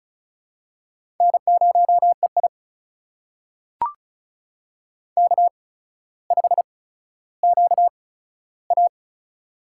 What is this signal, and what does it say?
Morse code, heard as a keyed tone, about 36 wpm, spaced as for 15 wpm, 700 Hz. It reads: N0EI K5QA
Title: USA Call Signs - 4 Characters - 8x Character Spacing - 35wpm